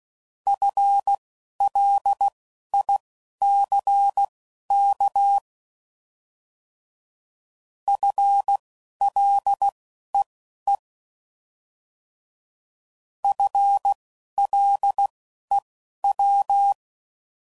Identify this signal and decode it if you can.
Morse code, heard as a keyed tone, about 16 wpm, spaced as for 11 wpm, 800 Hz. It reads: FLICK FLEE FLEW